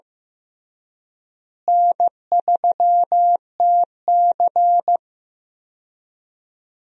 Morse code, heard as a keyed tone, about 15 wpm, 700 Hz. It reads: N3TC
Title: USA Call Signs - 4 Characters - Repeated 3x - 15wpm